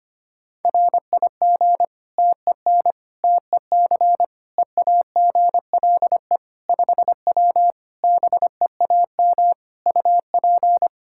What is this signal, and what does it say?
Morse code, heard as a keyed tone, about 25 wpm, 700 Hz. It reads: RIG TEN TEC EAGLE 5W BEAM UP
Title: Practice QSO - Exchange 2 - Info - 25wpm